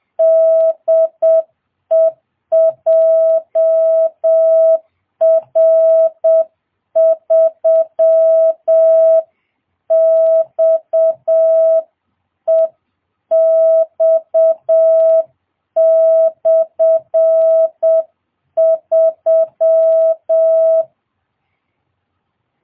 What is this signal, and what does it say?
Morse code, heard as a keyed tone, about 7 wpm, 660 Hz. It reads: DE JR3XEX/3